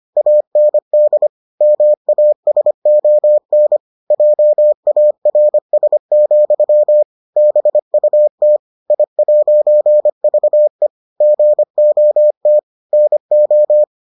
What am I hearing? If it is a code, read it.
Morse code, heard as a keyed tone, about 25 wpm, 600 Hz. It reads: AND MASON JARS, BUT I'VE GOT NO